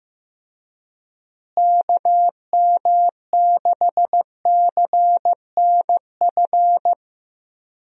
Morse code, heard as a keyed tone, about 15 wpm, 700 Hz. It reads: KM6CNF
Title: USA Call Signs - 6 Characters - Repeated 2x - 15wpm